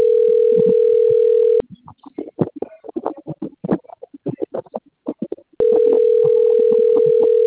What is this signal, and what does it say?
Ringback tone (North America), 440 + 480 Hz.